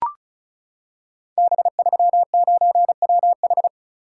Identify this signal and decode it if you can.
Morse code, heard as a keyed tone, about 36 wpm, 690 Hz. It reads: B39WH
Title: Code Groups - 5 Letter:Numbers - 35wpm